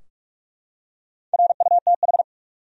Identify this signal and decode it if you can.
Morse code, heard as a keyed tone, about 43 wpm, 700 Hz. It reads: RUTH
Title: USA Common Names - Max 4 Letters - 45wpm